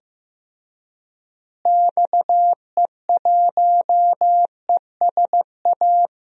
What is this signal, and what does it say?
Morse code, heard as a keyed tone, about 15 wpm, 700 Hz. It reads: XE1ESA
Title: DXCC-100 Call Signs - 6 Characters - Repeated 2x - 15wpm